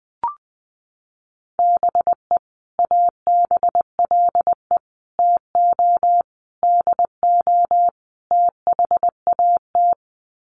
Morse code, heard as a keyed tone, about 20 wpm, 700 Hz. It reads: BE ABLE TO DO THAT
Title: Sets of 5 Words Encore2 20wpm